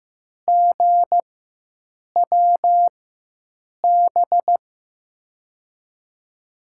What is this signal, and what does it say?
Morse code, heard as a keyed tone, about 15 wpm, spaced as for 7 wpm, 700 Hz. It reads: GWB
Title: USA Call Signs - 6 Characters - 4x Character Spacing - 15wpm